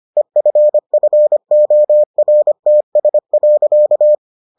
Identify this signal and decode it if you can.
Morse code, heard as a keyed tone, about 25 wpm, 600 Hz. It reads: EFFORTS.